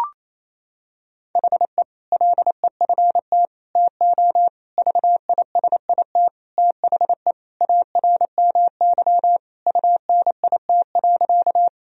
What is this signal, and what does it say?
Morse code, heard as a keyed tone, about 28 wpm, 700 Hz. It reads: HE LEFT TO VISIT THE ARMY UNIT.
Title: Sentences from Top 1000 Words - Review Part 2 - 28wpm